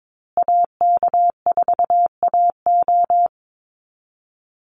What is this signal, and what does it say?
Morse code, heard as a keyed tone, about 22 wpm, 700 Hz. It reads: AK4AO
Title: Call Signs Repeated 3x - 22wpm